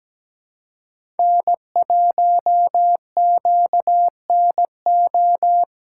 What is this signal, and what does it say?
Morse code, heard as a keyed tone, about 17 wpm, 700 Hz. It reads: N1QNO
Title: USA Call Signs - 5 Characters - 3x Character Spacing - 17wpm